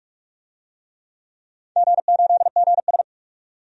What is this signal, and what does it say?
Morse code, heard as a keyed tone, about 46 wpm, 700 Hz. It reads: G8GS
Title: DXCC-100 Call Signs - 4 Characters - 45wpm